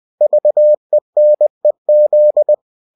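Morse code, heard as a keyed tone, about 20 wpm, 600 Hz. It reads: VENEZ
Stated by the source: Bulletin sent in CW.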